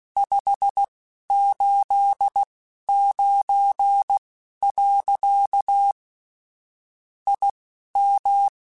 Morse code, heard as a keyed tone, about 16 wpm, spaced as for 11 wpm, 800 Hz. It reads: 589. IM
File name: QSOs / QSOs-mp3s / WD0FBT 1@10